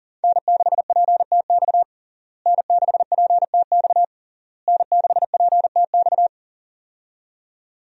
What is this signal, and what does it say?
Morse code, heard as a keyed tone, about 40 wpm, 700 Hz. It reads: N6PTX N6PTX N6PTX
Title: USA Call Signs - 5 Characters - Repeated 3x - 40wpm